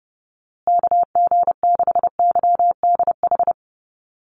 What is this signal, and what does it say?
Morse code, heard as a keyed tone, about 30 wpm, 700 Hz. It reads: KG6YDH
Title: USA Call Signs Repeated 3x 30wpm